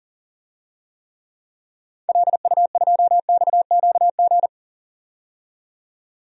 Morse code, heard as a keyed tone, about 40 wpm, 700 Hz. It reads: LU2XQG
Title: DXCC-100 Call Signs - 6 Characters - 40wpm